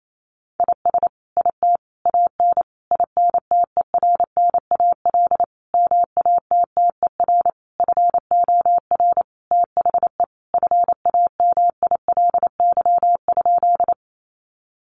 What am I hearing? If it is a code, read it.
Morse code, heard as a keyed tone, about 28 wpm, 700 Hz. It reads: IS IT AN INTERNAL MATTER FOR THE FAMILY?
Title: Sentences from Top 1400 Words 28wpm